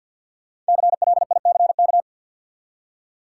Morse code, heard as a keyed tone, about 47 wpm, 700 Hz. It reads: CLICK
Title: Five Letter Words - 50wpm